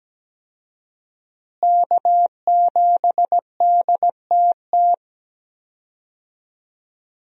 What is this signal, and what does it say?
Morse code, heard as a keyed tone, about 17 wpm, 700 Hz. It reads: K7DTT
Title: USA Call Signs - 5 Characters - Repeated 3x - 17wpm